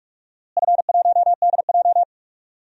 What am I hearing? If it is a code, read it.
Morse code, heard as a keyed tone, about 47 wpm, 700 Hz. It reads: F1DJ
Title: DXCC-100 Call Signs - 4 Characters - Repeated 3x - 45wpm